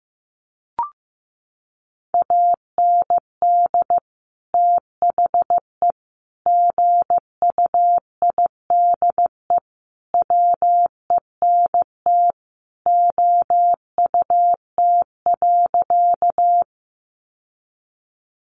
Morse code, heard as a keyed tone, about 15 wpm, 700 Hz. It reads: AND THE GUIDE WENT OUT.
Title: Sentences from Top 600 Words 15wpm